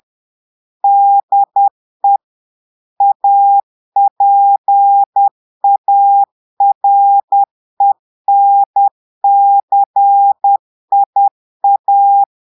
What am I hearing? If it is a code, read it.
Morse code, heard as a keyed tone, about 10 wpm, 800 Hz. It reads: DE APARENCIA
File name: Portugais - texte en clair 1 - vitesse - 10 mots minute